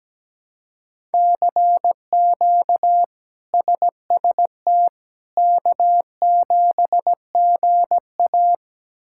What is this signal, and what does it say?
Morse code, heard as a keyed tone, about 17 wpm, 700 Hz. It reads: CQ SST K7GA
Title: K1USN SST QSO Practice - 17wpm